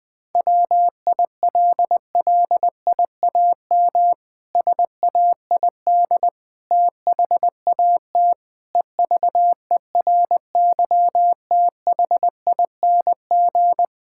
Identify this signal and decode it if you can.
Morse code, heard as a keyed tone, about 21 wpm, 700 Hz. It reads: WILLIAM SAID THAT EVERYTHING
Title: Pangram Challenge v2 - 20wpm